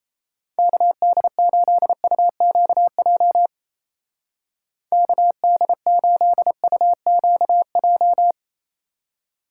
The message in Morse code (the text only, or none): KD8UQJ KD8UQJ